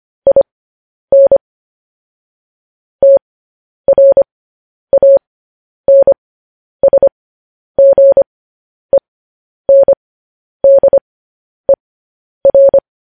Morse code, heard as a keyed tone, about 25 wpm, spaced as for 10 wpm, 550 Hz. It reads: IN TRANSGENDER
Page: news-headlines-in-morse-code-at-10-wpm